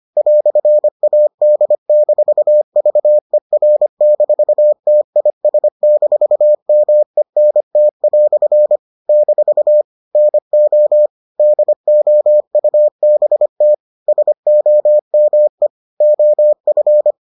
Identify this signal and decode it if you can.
Morse code, heard as a keyed tone, about 25 wpm, 600 Hz. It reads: "AD-VER-TIS-MENT" - NO DOUBT SOME OF